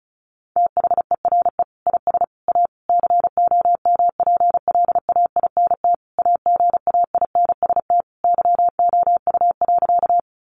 Morse code, heard as a keyed tone, about 34 wpm, 700 Hz. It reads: THERE IS A COMPLAINT AGAINST YOU.
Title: Sentences from Top 1400 Words 35wpm